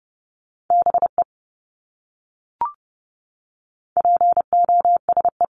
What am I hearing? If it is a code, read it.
Morse code, heard as a keyed tone, about 30 wpm, 700 Hz. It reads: BE POSE
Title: Top 500 Words Repeated 30wpm